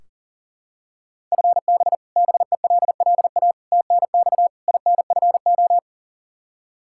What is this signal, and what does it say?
Morse code, heard as a keyed tone, about 40 wpm, 700 Hz. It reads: FB BELLA TNX INFO
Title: Practice QSO - Exchange 3 - More Info - 40wpm